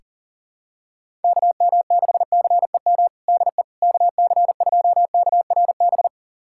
Morse code, heard as a keyed tone, about 40 wpm, 690 Hz. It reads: KM6CEM DE KC2KRB
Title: USA Callsign DE Callsign - 6 Characters - 40wpm